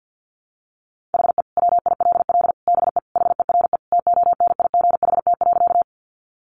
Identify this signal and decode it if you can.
Morse code, heard as a keyed tone, about 50 wpm, 700 Hz. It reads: HE WILL BE HERE TONIGHT.
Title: Sentences from Top 600 Words 50wpm